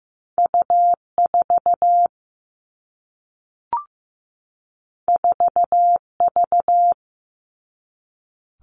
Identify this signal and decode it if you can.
Morse code, heard as a keyed tone, about 15 wpm, 700 Hz. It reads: U4 4V
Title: All Cut Numbers - 2-Digit 15wpm